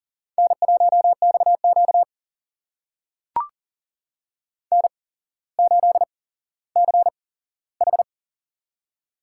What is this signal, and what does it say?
Morse code, heard as a keyed tone, about 39 wpm, spaced as for 17 wpm, 700 Hz. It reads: N1XQ N8CH
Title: USA Call Signs - 4 Characters - 8x Character Spacing - 40wpm